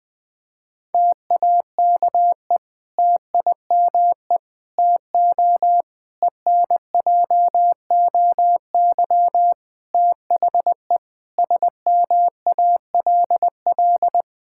Morse code, heard as a keyed tone, about 21 wpm, 700 Hz. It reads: TAKE TIME TO ENJOY THE SMALL
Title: Difficult Sentences with Hints - 20wpm